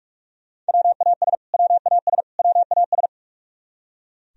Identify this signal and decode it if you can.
Morse code, heard as a keyed tone, about 46 wpm, 700 Hz. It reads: WAS WAS WAS
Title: Top 100 Words Repeated 45wpm